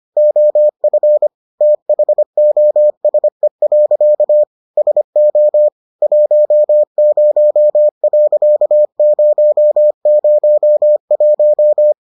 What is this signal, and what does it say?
Morse code, heard as a keyed tone, about 25 wpm, 600 Hz. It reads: OF THOSE. SO 10.001